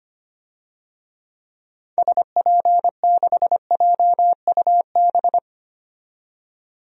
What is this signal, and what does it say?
Morse code, heard as a keyed tone, about 25 wpm, 700 Hz. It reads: SP6JUB
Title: DXCC-100 Call Signs - 6 Characters - 25wpm